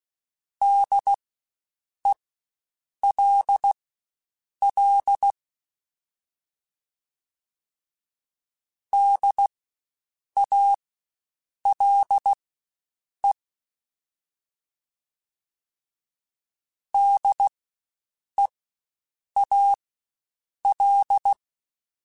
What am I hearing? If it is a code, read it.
Morse code, heard as a keyed tone, about 16 wpm, spaced as for 7 wpm, 800 Hz. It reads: DELL DALE DEAL
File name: Spaced Alphas / Spaced Alphas mp3s / Ds sp @15